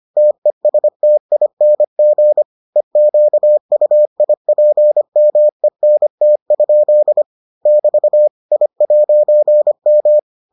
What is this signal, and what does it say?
Morse code, heard as a keyed tone, about 25 wpm, 600 Hz. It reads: TESTING EQUIPMENT? <BT> I'M